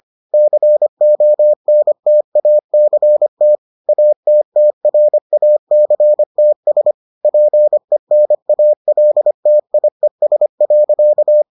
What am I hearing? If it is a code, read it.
Morse code, heard as a keyed tone, about 25 wpm, 600 Hz. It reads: CONTACT ATTRACTS PENALTIES.